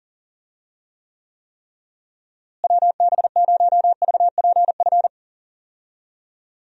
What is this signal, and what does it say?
Morse code, heard as a keyed tone, about 40 wpm, 700 Hz. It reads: WB0VPF
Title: USA Call Signs - 6 Characters - Repeated 3x - 40wpm